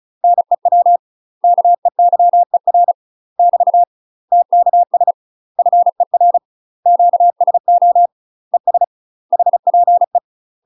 Morse code, heard as a keyed tone, about 36 wpm, 700 Hz. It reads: NEW KEYER <BT> TKS FER QSO ES HPE
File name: UK MASTER QSOs TRAINING / 35WPM / UK MASTER QSOs 0036